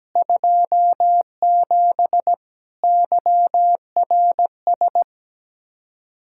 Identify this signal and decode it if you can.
Morse code, heard as a keyed tone, about 17 wpm, 700 Hz. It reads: 27 YRS
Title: Practice QSO - Exchange 3 - More Info - 17wpm